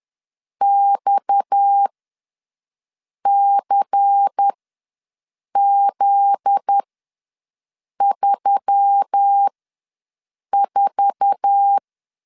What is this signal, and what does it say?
Morse code, heard as a keyed tone, about 10 wpm, spaced as for 5 wpm, 800 Hz. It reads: XCZ34
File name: LEZIONI / 9 2867 ws 10wpm 4215